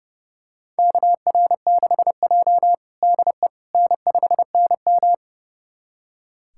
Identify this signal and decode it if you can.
Morse code, heard as a keyed tone, about 30 wpm, 700 Hz. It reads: KR6J DE N5NM
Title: USA Callsign DE Callsign - 4 Characters - 30wpm